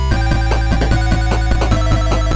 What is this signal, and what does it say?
There is no code.